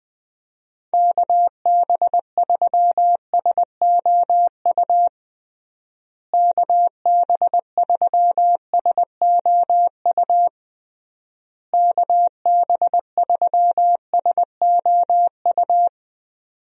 Morse code, heard as a keyed tone, about 20 wpm, 700 Hz. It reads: KB3SOU KB3SOU KB3SOU
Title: USA Call Signs - 6 Characters - Repeated 3x - 20wpm